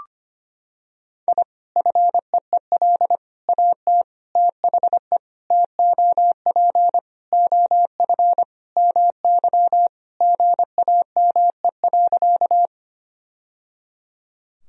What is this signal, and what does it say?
Morse code, heard as a keyed tone, about 25 wpm, 700 Hz. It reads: I FEEL AT THE TOP OF MY GAME.
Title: Sentences from Top 1000 Words - Review Part 2 - 25wpm